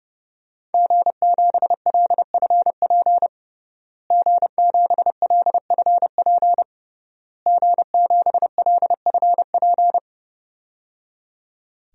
Morse code, heard as a keyed tone, about 30 wpm, 700 Hz. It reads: G7LFP G7LFP G7LFP
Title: DXCC-100 Call Signs - 5 Characters - Repeated 3x - 30wpm